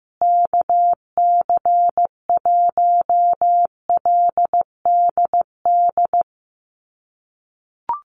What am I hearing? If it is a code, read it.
Morse code, heard as a keyed tone, about 15 wpm, 700 Hz. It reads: KC1LDD
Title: USA Call Signs Repeated 2x 15wpm